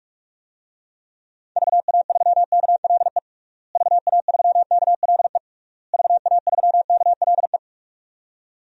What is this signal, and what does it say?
Morse code, heard as a keyed tone, about 46 wpm, 700 Hz. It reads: VA3KLE VA3KLE VA3KLE
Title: DXCC-100 Call Signs - 6 Characters - Repeated 3x - 45wpm